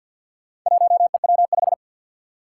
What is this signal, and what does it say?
Morse code, heard as a keyed tone, about 46 wpm, 700 Hz. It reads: 1EW5
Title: Four Challenging Letter-Number Combinations 50wpm